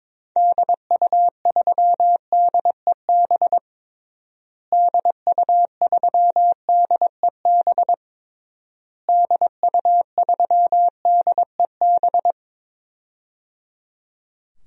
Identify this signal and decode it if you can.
Morse code, heard as a keyed tone, about 22 wpm, 700 Hz. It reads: DU3DEB DU3DEB DU3DEB
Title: DXCC-100 Call Signs - 6 Characters - Repeated 3x - 22wpm